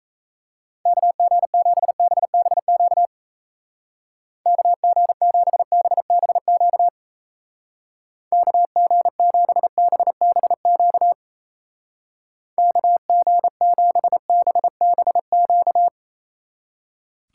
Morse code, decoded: KG7BBQ KG7BBQ KG7BBQ KG7BBQ

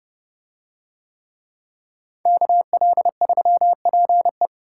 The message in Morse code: KL3PE

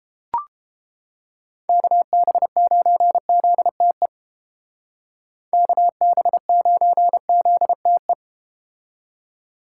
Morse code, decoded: KB9ZTE KB9ZTE

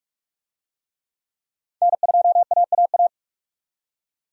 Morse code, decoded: N2AAA

